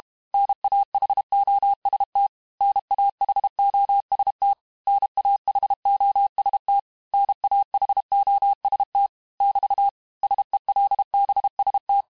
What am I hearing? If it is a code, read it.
Morse code, heard as a keyed tone, about 32 wpm, 800 Hz. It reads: NAHOST NAHOST NAHOST NAHOST <BT> SELBST
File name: cours-du-mercredi / audios / cours-868-teste-MOTS-vitesse-30mpm